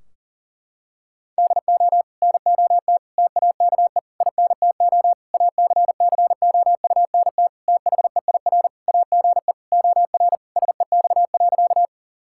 Morse code, heard as a keyed tone, about 40 wpm, 700 Hz. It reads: DO NOT TAKE INTO ACCOUNT THEIR AGE OR SEX.
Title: Sentences from Top 1600 Words 40wpm